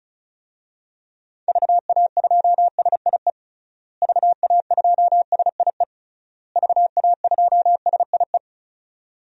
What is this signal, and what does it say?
Morse code, heard as a keyed tone, about 34 wpm, 700 Hz. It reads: VA2SIE VA2SIE VA2SIE
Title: DXCC-100 Call Signs - 6 Characters - Repeated 3x - 35wpm